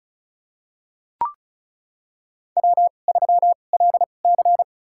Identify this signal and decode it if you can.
Morse code, heard as a keyed tone, about 36 wpm, spaced as for 25 wpm, 690 Hz. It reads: W3LC
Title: USA Call Signs - 4 Characters - 2x Character Spacing - 35wpm